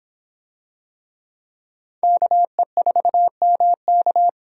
Morse code, heard as a keyed tone, about 25 wpm, 700 Hz. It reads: KE4MK